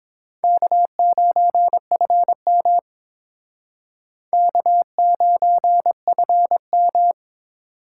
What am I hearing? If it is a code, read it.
Morse code, heard as a keyed tone, about 29 wpm, 700 Hz. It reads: K9FM K9FM